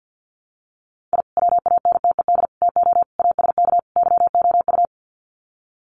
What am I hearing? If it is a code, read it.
Morse code, heard as a keyed tone, about 48 wpm, 700 Hz. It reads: I WANTED TO ASK YOU
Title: Sets of 5 Words Encore2 50wpm